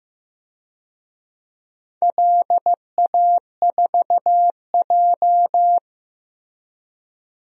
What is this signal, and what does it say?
Morse code, heard as a keyed tone, about 15 wpm, 700 Hz. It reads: LA4J
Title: DXCC-100 Call Signs - 4 Characters - 15wpm